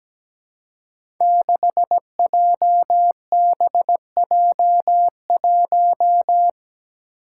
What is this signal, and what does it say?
Morse code, heard as a keyed tone, about 17 wpm, 700 Hz. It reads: 6JBJ1
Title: Five Challenging Letter-Number Combinations 17wpm